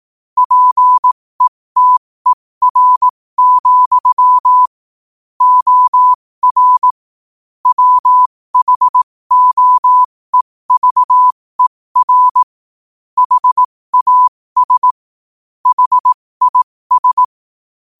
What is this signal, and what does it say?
Morse code, heard as a keyed tone, about 18 wpm, spaced as for 15 wpm, 1000 Hz. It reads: PETER, OR WHOEVER HAS HIS
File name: qotdfiles / 15wpm